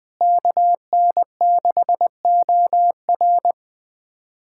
Morse code, heard as a keyed tone, about 20 wpm, 700 Hz. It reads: KN6OR
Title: USA Callsign DE Callsign 20wpm